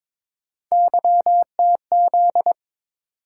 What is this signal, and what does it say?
Morse code, heard as a keyed tone, about 22 wpm, 700 Hz. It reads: YTZ
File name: practice / 22wpm / Three Letter-Number Combinations - 22wpm